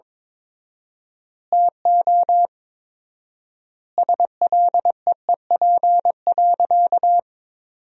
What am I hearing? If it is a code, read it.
Morse code, heard as a keyed tone, about 22 wpm, 700 Hz. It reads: TO SLEEP.
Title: 4x Word-Spacing - Sentences from Top 500 Words - 22wpm